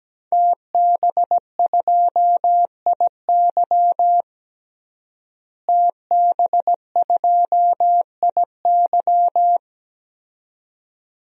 Morse code, heard as a keyed tone, about 17 wpm, 700 Hz. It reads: TB2IY TB2IY
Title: DXCC-100 Call Signs - 5 Characters - Repeated 2x - 17wpm